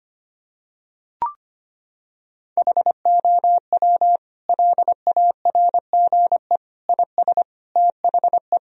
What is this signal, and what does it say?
Morse code, heard as a keyed tone, about 25 wpm, 700 Hz. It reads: HOW LARGE IS THE
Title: Sentences from Top 1000 Words - Review Part 4 - 25wpm